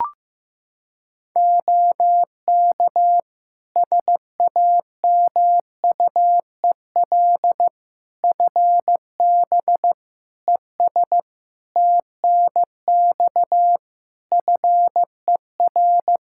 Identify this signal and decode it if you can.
Morse code, heard as a keyed tone, about 16 wpm, 700 Hz. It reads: OK SAMUEL FB ES TNX FER
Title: K1USN SST QSO Practice 15wpm